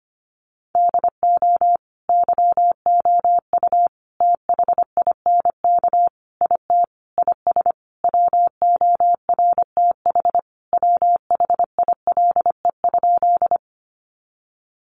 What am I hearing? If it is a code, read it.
Morse code, heard as a keyed tone, about 25 wpm, 700 Hz. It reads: DO YOU THINK IT IS WORTH WHILE?
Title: Sentences from Top 800 Words 25wpm